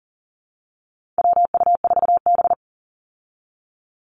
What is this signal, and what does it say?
Morse code, heard as a keyed tone, about 40 wpm, 700 Hz. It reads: WU4B
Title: Call Signs Repeated 2x - 40wpm